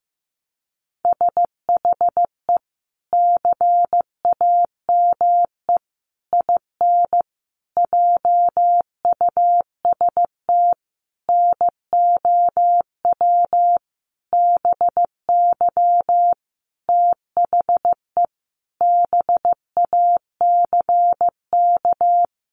Morse code, heard as a keyed tone, about 15 wpm, 700 Hz. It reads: SHE CAME IN JUST NOW BY THE BACK
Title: Sentences from Top 300 Words 15wpm